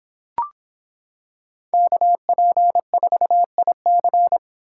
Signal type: Morse code

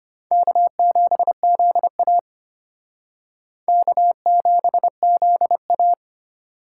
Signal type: Morse code

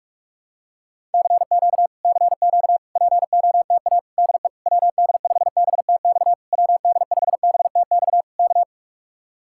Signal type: Morse code

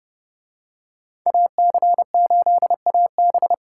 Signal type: Morse code

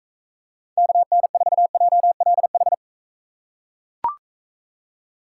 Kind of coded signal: Morse code

